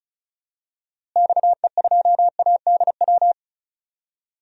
Morse code, heard as a keyed tone, about 36 wpm, 700 Hz. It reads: XE2ADW
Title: DXCC-100 Call Signs - 6 Characters - Repeated 3x - 35wpm